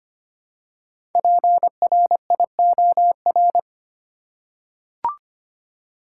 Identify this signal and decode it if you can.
Morse code, heard as a keyed tone, about 25 wpm, 700 Hz. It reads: PRIOR
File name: 5 Letter Words - First 3 Letters Given - 25wpm